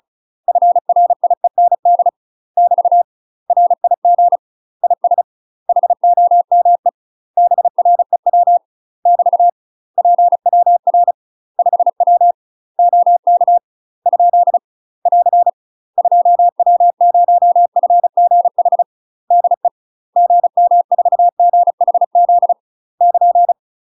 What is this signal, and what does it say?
Morse code, heard as a keyed tone, about 35 wpm, 700 Hz. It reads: FRIEND <BT> RIG IS HOME BREW <BT> PWR 5W OK ? <AR> 2W0FGH DE GM4GHZ <KN>
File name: UK MASTER QSOs TRAINING / 35WPM / UK MASTER QSOs 0024